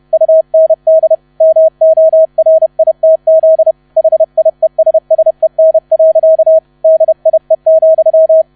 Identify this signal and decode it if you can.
Morse code, heard as a keyed tone, about 31 wpm, 640 Hz, with radio noise. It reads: UND MORITZ HIESSEN. DIE,
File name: 30wpm